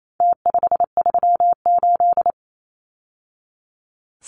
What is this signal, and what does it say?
Morse code, heard as a keyed tone, about 28 wpm, 700 Hz. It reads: T538
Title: Numbers - Contest Style - 4-Digit 28wpm